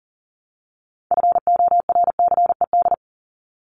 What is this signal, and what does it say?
Morse code, heard as a keyed tone, about 40 wpm, 700 Hz. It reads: FORCED
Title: ED Suffix - Top 100 Words with Base Words 40wpm